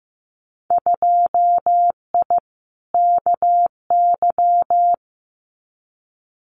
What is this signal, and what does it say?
Morse code, heard as a keyed tone, about 15 wpm, 700 Hz. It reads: 2I KY
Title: Winter Field Day Exchanges 15wpm